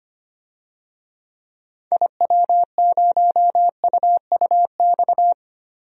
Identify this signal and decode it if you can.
Morse code, heard as a keyed tone, about 25 wpm, 700 Hz. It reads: IW0UUX
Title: DXCC-100 Call Signs - 6 Characters - Repeated 2x - 25wpm